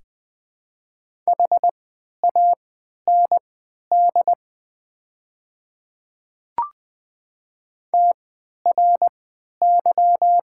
Morse code, heard as a keyed tone, about 20 wpm, spaced as for 11 wpm, 700 Hz. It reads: HAND TRY
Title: 3x Character Spacing - Top 300 Words - 20wpm